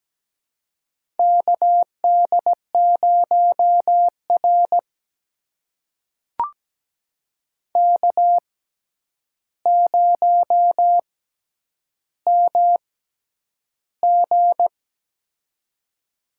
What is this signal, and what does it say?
Morse code, heard as a keyed tone, about 17 wpm, spaced as for 9 wpm, 700 Hz. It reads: KD0R K0MG
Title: USA Call Signs - 4 Characters - 6x Character Spacing - 17wpm